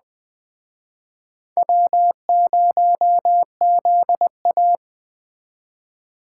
Morse code, heard as a keyed tone, about 21 wpm, 700 Hz. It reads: W0ZA